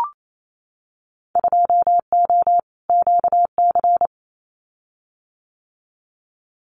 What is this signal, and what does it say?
Morse code, heard as a keyed tone, about 28 wpm, 700 Hz. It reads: 2O QC
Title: Winter Field Day Exchanges 28wpm